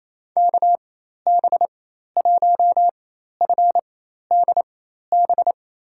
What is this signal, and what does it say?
Morse code, heard as a keyed tone, about 28 wpm, spaced as for 13 wpm, 700 Hz. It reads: KB1FDB
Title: USA Call Signs - 6 Characters - 4x Character Spacing - 28wpm